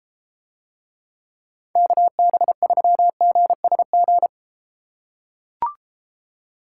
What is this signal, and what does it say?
Morse code, heard as a keyed tone, about 33 wpm, 700 Hz. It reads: KB3GSG